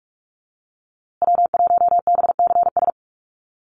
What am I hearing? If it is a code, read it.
Morse code, heard as a keyed tone, about 46 wpm, 690 Hz. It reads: W1BCS
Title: USA Call Signs Repeated 3x 45wpm